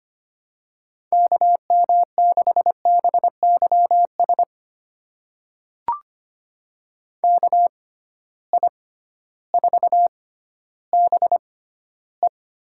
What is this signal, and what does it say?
Morse code, heard as a keyed tone, about 25 wpm, spaced as for 13 wpm, 700 Hz. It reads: KM6BYS KI4BE
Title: USA Call Signs - 6 Characters - 6x Character Spacing - 25wpm